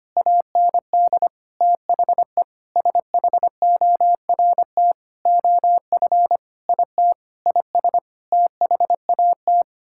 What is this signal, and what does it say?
Morse code, heard as a keyed tone, about 25 wpm, 700 Hz. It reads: AND THE SHORT OF IT IS THAT
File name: Sets of 9 Words - 25wpm